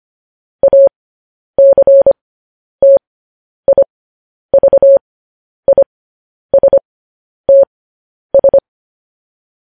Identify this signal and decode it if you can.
Morse code, heard as a keyed tone, about 25 wpm, spaced as for 10 wpm, 550 Hz. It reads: ACTIVISTS